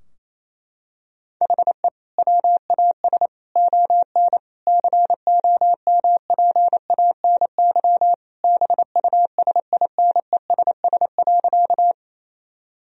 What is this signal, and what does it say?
Morse code, heard as a keyed tone, about 28 wpm, 700 Hz. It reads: HE WAS ON COMPANY BUSINESS.
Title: Sentences from Top 1000 Words - Review Part 3 - 28wpm